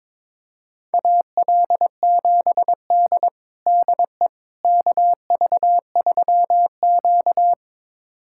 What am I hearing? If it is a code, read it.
Morse code, heard as a keyed tone, about 22 wpm, 700 Hz. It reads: AL7D DE KV3Q
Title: USA Callsign DE Callsign - 4 Characters - 22wpm